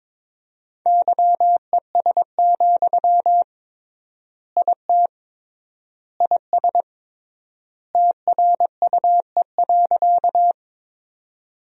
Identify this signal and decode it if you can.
Morse code, heard as a keyed tone, about 22 wpm, 700 Hz. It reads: YES, IT IS TRUE.
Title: 3x Word-Spacing - Sentences from Top 500 Words - 22wpm